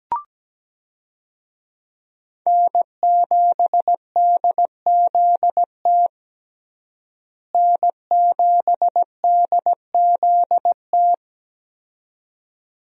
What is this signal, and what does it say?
Morse code, heard as a keyed tone, about 17 wpm, 700 Hz. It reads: N7DZT N7DZT
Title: USA Call Signs - 5 Characters - Repeated 2x - 17wpm